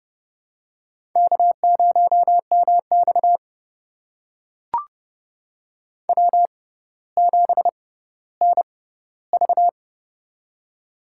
Morse code, heard as a keyed tone, about 32 wpm, spaced as for 16 wpm, 700 Hz. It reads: K0MX W7NV
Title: USA Call Signs - 6x Character Spacing - 30wpm